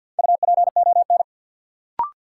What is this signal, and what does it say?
Morse code, heard as a keyed tone, about 53 wpm, 700 Hz. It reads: UPON
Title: Sentences from Top 1100 Words - New Word Drill Alt - 50wpm